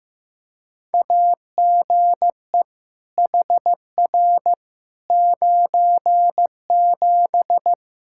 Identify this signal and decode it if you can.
Morse code, heard as a keyed tone, about 15 wpm, 700 Hz. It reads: AGE HR 97
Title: Practice QSO - Exchange 3 - More Info - 15wpm